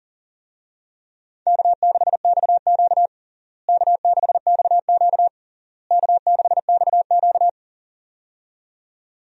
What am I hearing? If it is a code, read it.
Morse code, heard as a keyed tone, about 40 wpm, 700 Hz. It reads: K6XQ K6XQ K6XQ
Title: USA Call Signs - 4 Characters - Repeated 3x - 40wpm